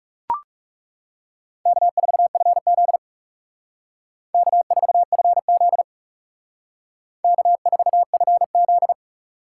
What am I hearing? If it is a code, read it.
Morse code, heard as a keyed tone, about 38 wpm, 700 Hz. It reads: K4FZ K4FZ K4FZ